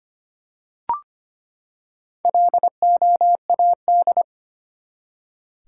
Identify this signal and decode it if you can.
Morse code, heard as a keyed tone, about 25 wpm, 700 Hz. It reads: LOAD
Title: Four Letter Words - 25wpm